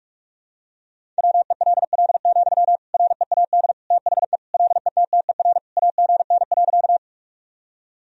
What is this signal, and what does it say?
Morse code, heard as a keyed tone, about 46 wpm, 700 Hz. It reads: WELL, READ THE LETTER AGN.
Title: Sentences from Top 300 Words and Abbreviations 45wpm